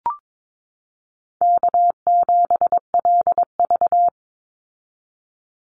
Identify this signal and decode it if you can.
Morse code, heard as a keyed tone, about 22 wpm, 700 Hz. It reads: K7LV
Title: Call Signs Repeated 2x - 22wpm